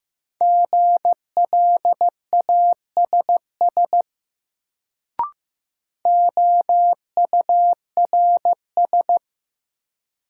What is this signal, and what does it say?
Morse code, heard as a keyed tone, about 15 wpm, 700 Hz. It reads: GLASS OURS
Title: Sentences from Top 1300 Words - New Word Drill Alt - 15wpm